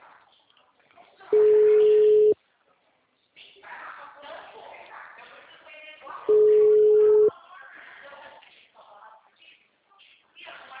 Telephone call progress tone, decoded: ringback tone

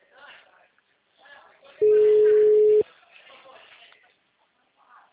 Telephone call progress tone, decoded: ringback tone

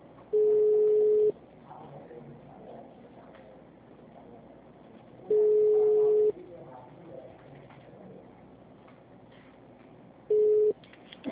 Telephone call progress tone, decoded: ringback tone

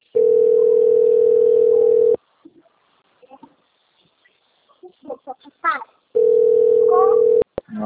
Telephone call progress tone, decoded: ringback tone

